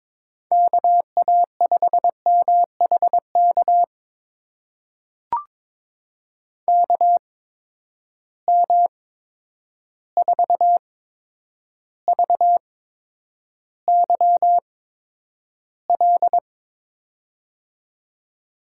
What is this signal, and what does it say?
Morse code, heard as a keyed tone, about 22 wpm, spaced as for 9 wpm, 700 Hz. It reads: KA5MHK KM4VYL
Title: USA Call Signs - 6 Characters - 8x Character Spacing - 22wpm